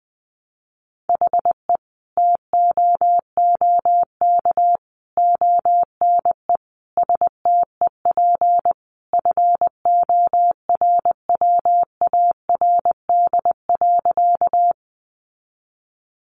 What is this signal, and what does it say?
Morse code, heard as a keyed tone, about 20 wpm, 700 Hz. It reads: HE TOOK ONE STEP FORWARD.
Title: Sentences from Top 1300 Words 20wpm